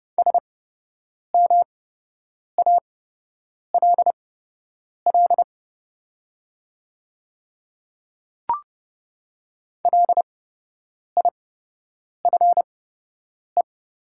Morse code, heard as a keyed tone, about 30 wpm, spaced as for 8 wpm, 700 Hz. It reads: SMALL LIFE
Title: 8x Character Spacing - Top 300 Words - 30wpm